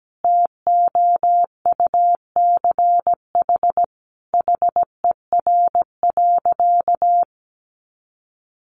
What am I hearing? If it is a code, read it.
Morse code, heard as a keyed tone, about 17 wpm, 700 Hz. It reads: TOUCH HER.
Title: Sentences from Top 1200 Words 17wpm